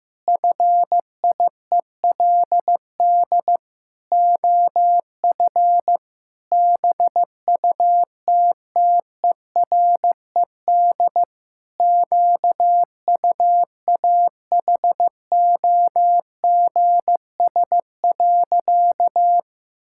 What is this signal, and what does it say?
Morse code, heard as a keyed tone, about 16 wpm, 700 Hz. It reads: FIELD OF BUTTERED QUAHOGS.
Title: Pangram Challenge v2 - 15wpm